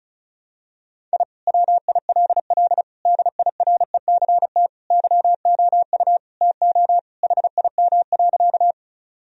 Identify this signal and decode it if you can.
Morse code, heard as a keyed tone, about 34 wpm, 700 Hz. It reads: I WILL DIRECT YOU TO HIM.
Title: Sentences from Top 1000 Words - Review Part 2 - 35wpm